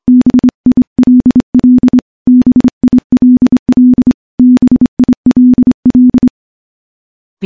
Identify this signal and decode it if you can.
Morse code, heard as a keyed tone, about 26 wpm, 250 Hz. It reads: BILL BILL BILL
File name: morse-training-25WPM